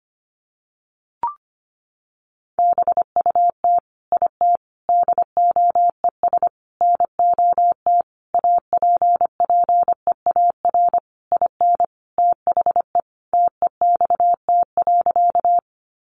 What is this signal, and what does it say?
Morse code, heard as a keyed tone, about 25 wpm, 700 Hz. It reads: BUT IT DOES NOT APPEAR IN THE TEXT.
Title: Sentences from Top 1200 Words 25wpm